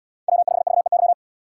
Morse code, heard as a keyed tone, about 70 wpm, 700 Hz. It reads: W5VEY